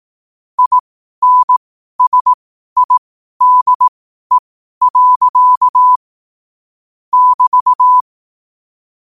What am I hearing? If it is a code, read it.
Morse code, heard as a keyed tone, about 18 wpm, spaced as for 12 wpm, 1000 Hz. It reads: INSIDE. <BT>
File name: qotdfiles / 12wpm